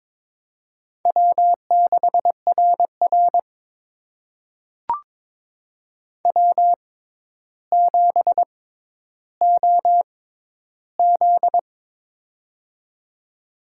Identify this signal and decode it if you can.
Morse code, heard as a keyed tone, about 21 wpm, spaced as for 11 wpm, 700 Hz. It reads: W6RR W7OZ
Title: USA Call Signs - 4 Characters - 6x Character Spacing - 22wpm